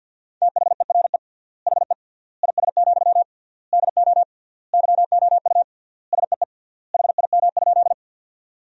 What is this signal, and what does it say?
Morse code, heard as a keyed tone, about 50 wpm, 700 Hz. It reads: THERE HE IS, DO YOU SEE HIM?
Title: 3x Word-Spacing - Sentences from Top 100 Words - 50wpm